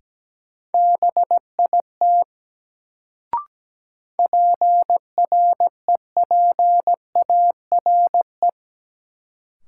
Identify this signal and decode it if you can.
Morse code, heard as a keyed tone, about 17 wpm, 700 Hz. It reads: BIT PREPARE
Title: Sentences from Top 900 Words - New Word Drill - 17wpm